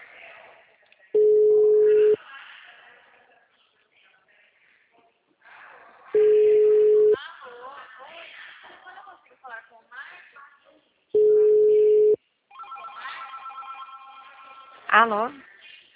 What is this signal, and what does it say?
Ringback tone (Europe, 425 Hz), with speech.
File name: out- -2050- - - .20866